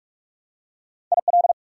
Morse code, heard as a keyed tone, about 46 wpm, 700 Hz. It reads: IL
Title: US States and Canadian Provinces 45wpm